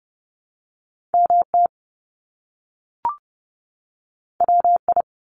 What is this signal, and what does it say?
Morse code, heard as a keyed tone, about 30 wpm, 700 Hz. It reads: MT WI
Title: US States 30wpm